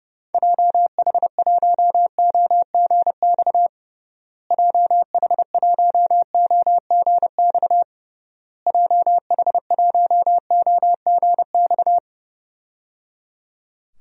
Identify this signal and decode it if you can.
Morse code, heard as a keyed tone, about 30 wpm, 700 Hz. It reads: JH1OGX JH1OGX JH1OGX
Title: DXCC-100 Call Signs - 6 Characters - Repeated 3x - 30wpm